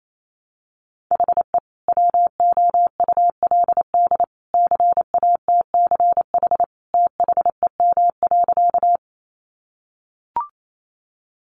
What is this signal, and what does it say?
Morse code, heard as a keyed tone, about 28 wpm, 700 Hz. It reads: HE WOULD CATCH THEM.
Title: Sentences from Top 1500 Words 28wpm